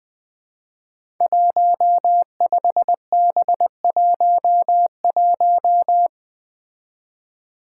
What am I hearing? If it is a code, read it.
Morse code, heard as a keyed tone, about 20 wpm, 700 Hz. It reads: 15B11
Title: Five Challenging Letter-Number Combinations 20wpm